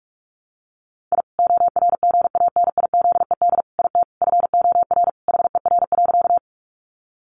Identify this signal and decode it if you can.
Morse code, heard as a keyed tone, about 44 wpm, 700 Hz. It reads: I ORGANIZED IT FOR HER.
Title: Sentences from Top 1300 Words 45wpm